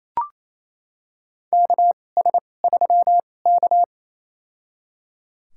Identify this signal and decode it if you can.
Morse code, heard as a keyed tone, about 27 wpm, spaced as for 20 wpm, 700 Hz. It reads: KS3K
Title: USA Call Signs - 2x Character Spacing - 28wpm